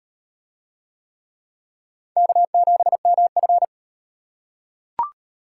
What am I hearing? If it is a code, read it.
Morse code, heard as a keyed tone, about 35 wpm, 700 Hz. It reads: K7MF